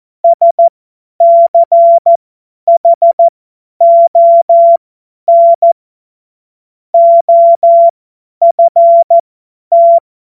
Morse code, heard as a keyed tone, about 14 wpm, spaced as for 10 wpm, 680 Hz. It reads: SCHON OFT
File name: Morsekurs / Witze / Witz 27. Dezember 2022 50 F70